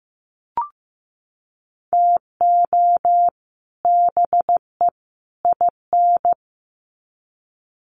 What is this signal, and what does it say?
Morse code, heard as a keyed tone, about 15 wpm, 700 Hz. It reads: TO BE IN
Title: Sets of 3 Words 15wpm